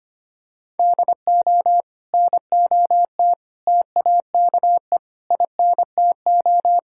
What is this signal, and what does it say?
Morse code, heard as a keyed tone, about 25 wpm, 700 Hz. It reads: DO NOT TAKE INTO
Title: Sentences from Top 1600 Words 25wpm